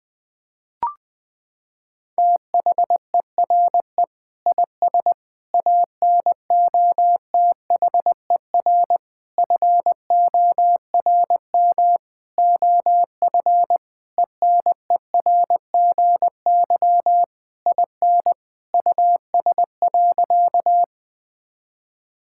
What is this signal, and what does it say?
Morse code, heard as a keyed tone, about 21 wpm, 700 Hz. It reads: THERE IS ANOTHER FORM OF ENERGY IN US.
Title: Sentences from Top 1000 Words - Review Part 6 - 20wpm